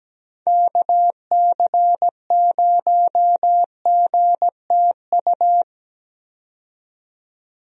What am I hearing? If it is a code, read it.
Morse code, heard as a keyed tone, about 17 wpm, 700 Hz. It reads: KC0GTU
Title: USA Call Signs - 6 Characters - 17wpm